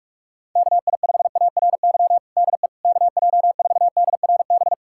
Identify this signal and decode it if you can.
Morse code, heard as a keyed tone, about 46 wpm, 700 Hz. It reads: KI5ARY DE KJ4DRB
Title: USA Callsign DE Callsign 45wpm